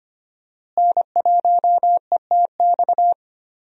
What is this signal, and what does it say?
Morse code, heard as a keyed tone, about 25 wpm, 700 Hz. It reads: N1ETX
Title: USA Call Signs - 5 Characters - Repeated 3x - 25wpm